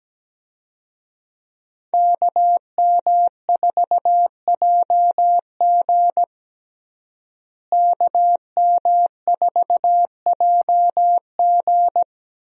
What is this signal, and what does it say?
Morse code, heard as a keyed tone, about 17 wpm, 700 Hz. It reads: KM4JG KM4JG
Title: USA Call Signs - 5 Characters - Repeated 2x - 17wpm